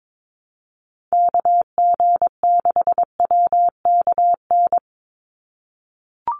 Morse code, heard as a keyed tone, about 22 wpm, 700 Hz. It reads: KG6WKN
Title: USA Call Signs 22wpm